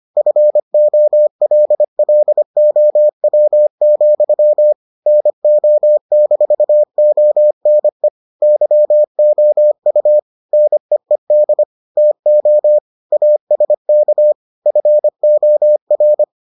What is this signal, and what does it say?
Morse code, heard as a keyed tone, about 25 wpm, 600 Hz. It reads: FOLLOW, NO-ONE YOU NEED TO ASK FOR